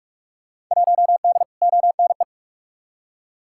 Morse code, heard as a keyed tone, about 47 wpm, 700 Hz. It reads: 1D ONE
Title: Field Day Exchanges - 45wpm